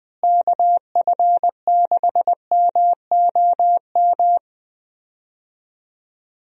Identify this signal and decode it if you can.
Morse code, heard as a keyed tone, about 21 wpm, 700 Hz. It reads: KF6MOM
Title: USA Call Signs - 6 Characters - Repeated 3x - 20wpm